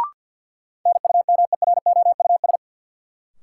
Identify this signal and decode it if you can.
Morse code, heard as a keyed tone, about 53 wpm, 700 Hz. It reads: NUMEROUS
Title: Sentences from Top 1400 Words - New Word Drill - 50wpm